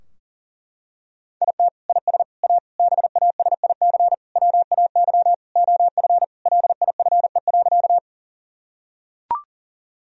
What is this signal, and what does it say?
Morse code, heard as a keyed tone, about 40 wpm, 700 Hz. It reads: IT IS A BASIC WAY OF LIFE.
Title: Sentences from Top 1000 Words - Review Part 3 - 40wpm